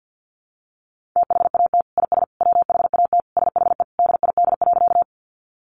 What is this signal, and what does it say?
Morse code, heard as a keyed tone, about 50 wpm, 700 Hz. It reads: THAT IS WHAT SHE DID.
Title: Sentences from Top 100 Words 50wpm